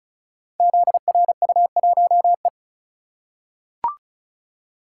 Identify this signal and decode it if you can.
Morse code, heard as a keyed tone, about 36 wpm, 700 Hz. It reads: ZRU1E
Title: Code Groups - 5 Letter:Numbers - 35wpm